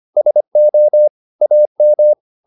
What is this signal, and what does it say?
Morse code, heard as a keyed tone, about 25 wpm, 600 Hz. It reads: SO AM